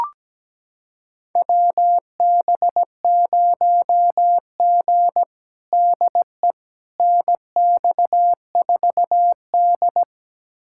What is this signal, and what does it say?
Morse code, heard as a keyed tone, about 17 wpm, 700 Hz. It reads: WB0G DE NX4D
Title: USA Callsign DE Callsign - 4 Characters - 17wpm